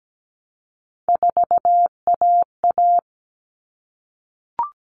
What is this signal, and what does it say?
Morse code, heard as a keyed tone, about 17 wpm, 700 Hz. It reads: 4AA
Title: All Cut Numbers - 3-Digit 17wpm